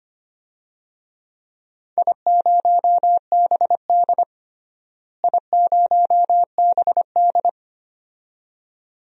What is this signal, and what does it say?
Morse code, heard as a keyed tone, about 25 wpm, 700 Hz. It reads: I0BD I0BD
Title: DXCC-100 Call Signs - 4 Characters - Repeated 2x - 25wpm